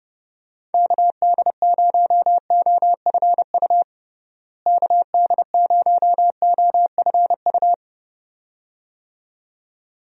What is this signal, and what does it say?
Morse code, heard as a keyed tone, about 32 wpm, 700 Hz. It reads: KD0OFU KD0OFU
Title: USA Call Signs - 6 Characters - Repeated 2x - 30wpm